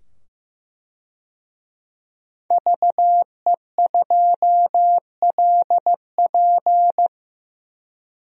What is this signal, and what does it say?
Morse code, heard as a keyed tone, about 15 wpm, 700 Hz. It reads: VE2LP
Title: DXCC-100 Call Signs - 5 Characters - 15wpm